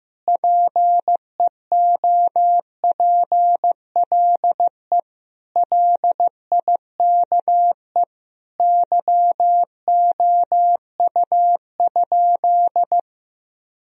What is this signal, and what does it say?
Morse code, heard as a keyed tone, about 15 wpm, 700 Hz. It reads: PEOPLE LIKE YOU?
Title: Sentences from Top 1000 Words - Review Part 1 - 15wpm